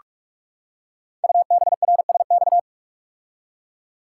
Morse code, heard as a keyed tone, about 44 wpm, 700 Hz. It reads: UBRSX
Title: Code Groups - 5 Letters - 45wpm